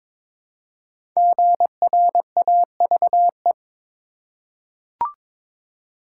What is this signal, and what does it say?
Morse code, heard as a keyed tone, about 22 wpm, 700 Hz. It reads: GRAVE
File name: 5 Letter Words - First 3 Letters Given - 22wpm